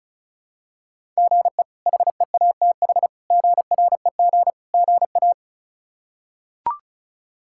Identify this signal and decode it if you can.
Morse code, heard as a keyed tone, about 36 wpm, 700 Hz. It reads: GE HEATH GREG GA
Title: K1USN SST QSO Practice - 35wpm